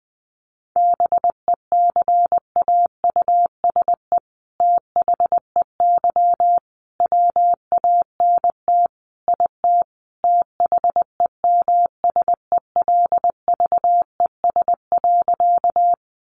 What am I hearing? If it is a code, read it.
Morse code, heard as a keyed tone, about 20 wpm, 700 Hz. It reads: BECAUSE THEY WANT IT THEMSELVES.
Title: Sentences from Top 800 Words 20wpm